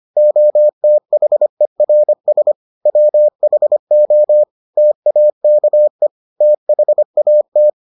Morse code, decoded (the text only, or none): OTHERS WHO TAKE THAT